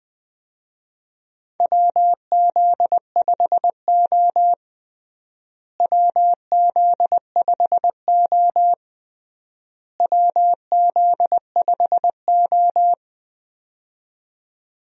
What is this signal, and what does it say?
Morse code, heard as a keyed tone, about 21 wpm, 700 Hz. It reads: WZ5O WZ5O WZ5O
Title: USA Call Signs - 4 Characters - Repeated 3x - 20wpm